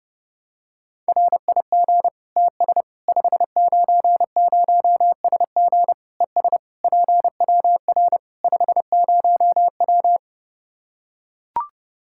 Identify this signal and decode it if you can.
Morse code, heard as a keyed tone, about 30 wpm, 700 Hz. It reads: RIG TS 590SG ES PWR 50W
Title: Practice QSO - Exchange 2 - Info - 30wpm